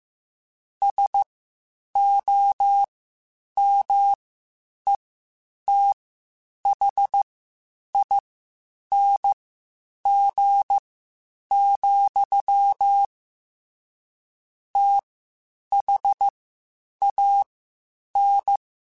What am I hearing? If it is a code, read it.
Morse code, heard as a keyed tone, about 15 wpm, spaced as for 8 wpm, 800 Hz. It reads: SOMETHING, THAN
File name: Quotes Spaced / Quotes Spaced-mp3s / StandFor sp @13